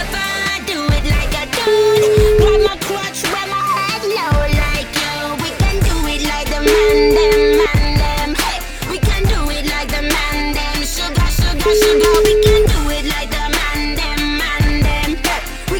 Ringback tone (Europe, 425 Hz), with noise.